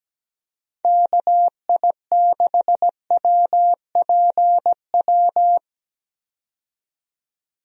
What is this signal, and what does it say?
Morse code, heard as a keyed tone, about 17 wpm, 700 Hz. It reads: KI6WPW
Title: USA Call Signs - 6 Characters - 17wpm